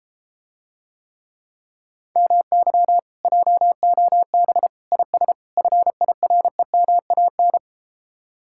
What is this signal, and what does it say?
Morse code, heard as a keyed tone, about 33 wpm, 700 Hz. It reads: MY JOB IS FIREMAN